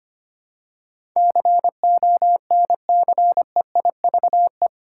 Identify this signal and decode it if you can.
Morse code, heard as a keyed tone, about 25 wpm, 700 Hz. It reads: CONCEIVE
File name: Eight Letter Words - 25wpm